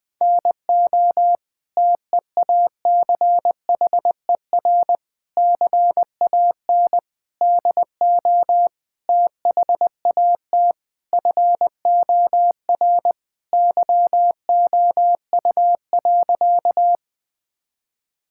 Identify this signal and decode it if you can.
Morse code, heard as a keyed tone, about 20 wpm, 700 Hz. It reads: NO TEACHER CAN DO THAT FOR YOU.
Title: Sentences from Top 1000 Words - Review Part 6 - 20wpm